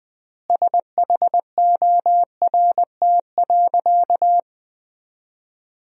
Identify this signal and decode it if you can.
Morse code, heard as a keyed tone, about 21 wpm, 700 Hz. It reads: SHORT.
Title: 3x Word-Spacing - Sentences from Top 500 Words - 20wpm